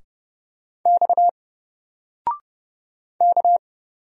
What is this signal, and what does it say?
Morse code, heard as a keyed tone, about 30 wpm, 700 Hz. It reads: X K
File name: Single-Letter-Number-30wpm